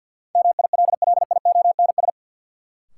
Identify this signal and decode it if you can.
Morse code, heard as a keyed tone, about 52 wpm, 700 Hz. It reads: MILLIONS
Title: Sets of 5 Words - New Word Drill - 50wpm